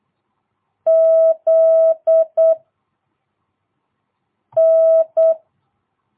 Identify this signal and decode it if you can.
Morse code, heard as a keyed tone, about 8 wpm, 650 Hz. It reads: Z N
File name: jh8cbh / 2022cw